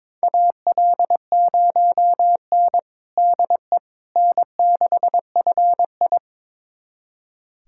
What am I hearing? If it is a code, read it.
Morse code, heard as a keyed tone, about 21 wpm, 700 Hz. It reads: AL0N DE N6FI
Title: USA Callsign DE Callsign - 4 Characters - 22wpm